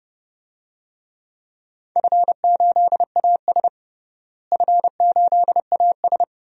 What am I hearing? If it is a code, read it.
Morse code, heard as a keyed tone, about 32 wpm, 700 Hz. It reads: F8AS F8AS
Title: DXCC-100 Call Signs - 4 Characters - Repeated 3x - 30wpm